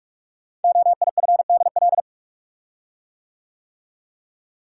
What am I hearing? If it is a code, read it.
Morse code, heard as a keyed tone, about 44 wpm, 700 Hz. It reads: OIFDL
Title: Code Groups - 5 Letters - 45wpm